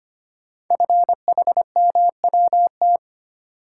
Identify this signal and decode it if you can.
Morse code, heard as a keyed tone, about 25 wpm, 700 Hz. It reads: FHMWT
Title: Code Groups - 5 Letters - 25wpm